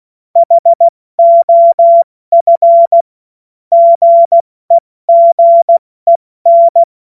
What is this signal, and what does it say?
Morse code, heard as a keyed tone, about 16 wpm, 680 Hz. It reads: HOF GEGEN
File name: Morsekurs / Witze / Witz 02. Mai 2023 70 F80